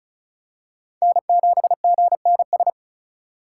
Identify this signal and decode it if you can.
Morse code, heard as a keyed tone, about 34 wpm, 700 Hz. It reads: N7GNS
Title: USA Call Signs - 5 Characters - 4x Character Spacing - 35wpm